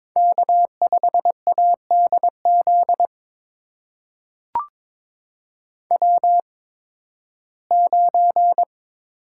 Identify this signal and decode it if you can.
Morse code, heard as a keyed tone, about 22 wpm, spaced as for 14 wpm, 700 Hz. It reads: K5ADZ W9
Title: USA Call Signs - 5 Characters - 8x Character Spacing - 22wpm